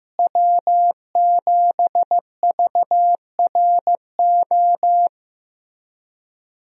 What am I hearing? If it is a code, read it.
Morse code, heard as a keyed tone, about 15 wpm, 700 Hz. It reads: W7VRO
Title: USA Call Signs - 5 Characters - Repeated 3x - 15wpm